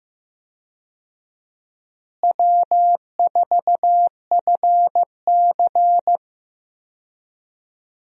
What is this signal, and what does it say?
Morse code, heard as a keyed tone, about 15 wpm, 700 Hz. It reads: W4FC
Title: USA Call Signs - 4 Characters - Repeated 2x - 15wpm